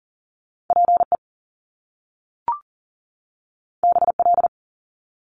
Morse code, heard as a keyed tone, about 40 wpm, 700 Hz. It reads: PE BL
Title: Top Two Letter Combinations in Words 40wpm